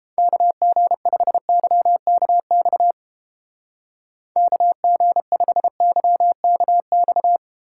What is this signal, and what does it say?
Morse code, heard as a keyed tone, about 30 wpm, 700 Hz. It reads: KG5YKX KG5YKX